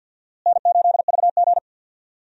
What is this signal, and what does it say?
Morse code, heard as a keyed tone, about 52 wpm, 700 Hz. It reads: N8VG
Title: USA Call Signs - 4 Characters - Repeated 2x - 50wpm